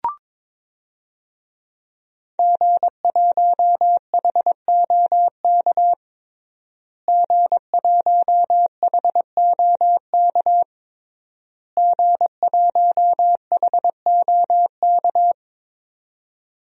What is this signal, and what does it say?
Morse code, heard as a keyed tone, about 22 wpm, 700 Hz. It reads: G1HOK G1HOK G1HOK
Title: DXCC-100 Call Signs - 5 Characters - Repeated 3x - 22wpm